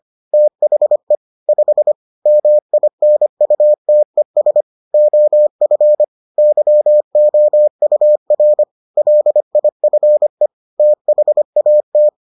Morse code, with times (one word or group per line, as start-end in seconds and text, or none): 0.3-1.1 THE
1.5-1.9 5
2.3-4.6 MINUTES
4.9-6.0 OF
6.4-8.6 YOUR
9.0-10.5 LIFE
10.8-12.1 THAT